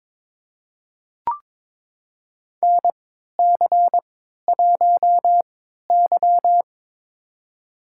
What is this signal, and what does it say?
Morse code, heard as a keyed tone, about 22 wpm, spaced as for 13 wpm, 700 Hz. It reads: NC1Y
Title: USA Call Signs - 4 Characters - 3x Character Spacing - 22wpm